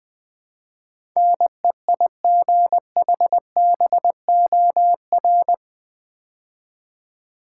Morse code, heard as a keyed tone, about 20 wpm, 700 Hz. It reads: NEIGHBOR
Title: Build A Word - 4 Letters At A Time - 11-Letter Words - 20wpm